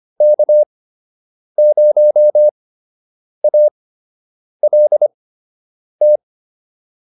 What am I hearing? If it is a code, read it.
Morse code, heard as a keyed tone, about 25 wpm, spaced as for 8 wpm, 600 Hz. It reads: K0ALT